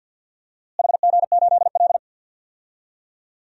Morse code, heard as a keyed tone, about 46 wpm, 700 Hz. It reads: HG8L
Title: DXCC-100 Call Signs - 4 Characters - Repeated 3x - 50wpm